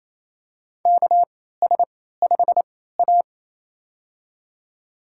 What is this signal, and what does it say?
Morse code, heard as a keyed tone, about 28 wpm, spaced as for 16 wpm, 700 Hz. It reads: KS5A
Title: USA Call Signs - 4 Characters - 3x Character Spacing - 28wpm